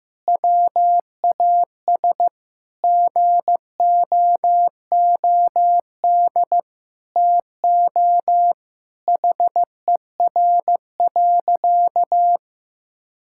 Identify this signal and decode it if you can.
Morse code, heard as a keyed tone, about 14 wpm, 700 Hz. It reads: WAS GOOD TO HER.
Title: Sentences from Top 1000 Words - Review Part 5 - 15wpm